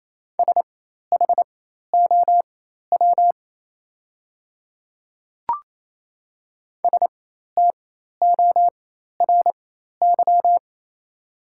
Morse code, heard as a keyed tone, about 28 wpm, spaced as for 13 wpm, 700 Hz. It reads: SHOW STORY
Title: 4x Character Spacing - Top 500 Words - 28wpm